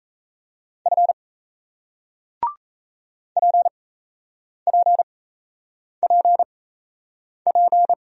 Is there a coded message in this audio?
Yes: Morse code from 0.9 to 7.9 s, heard as a keyed tone, about 38 wpm, spaced as for 8 wpm, 700 Hz. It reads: F PPPP